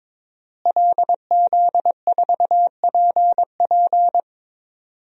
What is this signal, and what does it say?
Morse code, heard as a keyed tone, about 22 wpm, 700 Hz. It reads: LZ4PP
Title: DXCC-100 Call Signs - 5 Characters - Repeated 3x - 22wpm